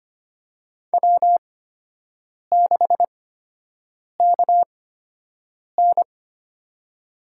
Morse code, heard as a keyed tone, about 25 wpm, spaced as for 7 wpm, 700 Hz. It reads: W6KN
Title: USA Call Signs - 4 Characters - 8x Character Spacing - 25wpm